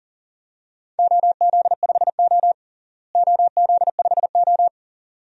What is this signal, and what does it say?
Morse code, heard as a keyed tone, about 40 wpm, 700 Hz. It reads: OZ5O OZ5O
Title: DXCC-100 Call Signs - 4 Characters - Repeated 2x - 40wpm